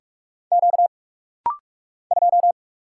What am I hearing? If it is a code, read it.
Morse code, heard as a keyed tone, about 46 wpm, 700 Hz. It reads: Q 2
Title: Single Letter-Number Alt - Rapid-Fire 45wpm